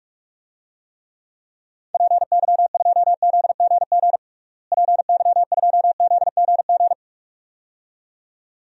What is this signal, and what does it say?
Morse code, heard as a keyed tone, about 41 wpm, 700 Hz. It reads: PY2ZGG PY2ZGG
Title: DXCC-100 Call Signs - 6 Characters - Repeated 2x - 45wpm